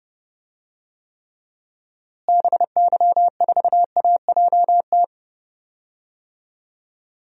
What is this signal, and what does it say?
Morse code, heard as a keyed tone, about 30 wpm, 700 Hz. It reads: BY4AJT
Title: DXCC-100 Call Signs - 6 Characters - Repeated 2x - 30wpm